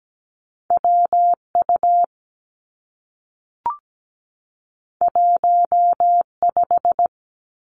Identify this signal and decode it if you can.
Morse code, heard as a keyed tone, about 17 wpm, 700 Hz. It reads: W U 1 5
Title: Two Challenging Letter-Number Combinations 17wpm